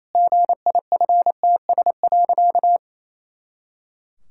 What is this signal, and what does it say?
Morse code, heard as a keyed tone, about 28 wpm, 700 Hz. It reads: GIFTS.
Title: Pangram Challenge v2 - 28wpm